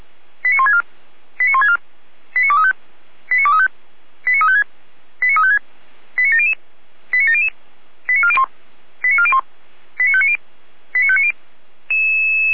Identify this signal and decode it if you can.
Five-tone selective call (ZVEI), heard as quick runs of five tones, tones of about 70 ms. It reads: 89164 89164 89235 89235 89356 89356 89700 89700 89401 89401 89590 89590